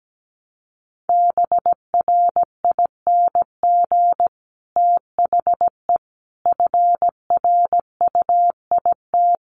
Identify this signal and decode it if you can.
Morse code, heard as a keyed tone, about 17 wpm, 700 Hz. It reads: BRING THE FRUIT
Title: Sentences from Top 1300 Words 17wpm